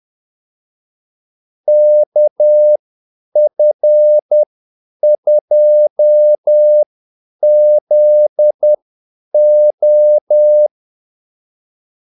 Morse code, heard as a keyed tone, about 10 wpm, spaced as for 8 wpm, 600 Hz. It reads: KF2ZO
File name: Callsigns-10-8wpm